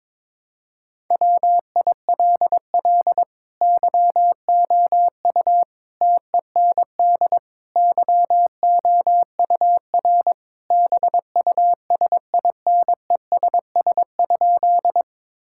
Morse code, heard as a keyed tone, about 22 wpm, 700 Hz. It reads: WILL YOU TEND YOUR BUSINESS?
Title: Sentences from Top 1000 Words - Review Part 5 - 22wpm